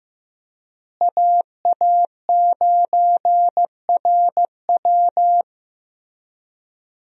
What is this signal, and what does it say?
Morse code, heard as a keyed tone, about 16 wpm, 700 Hz. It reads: AA9RW
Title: USA Call Signs - 5 Characters - Repeated 3x - 15wpm